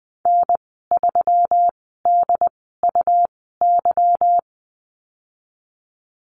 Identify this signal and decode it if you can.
Morse code, heard as a keyed tone, about 20 wpm, spaced as for 15 wpm, 700 Hz. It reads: N3DUY
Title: 207 Lesson 33g - Letter j - Call Signs 20wpm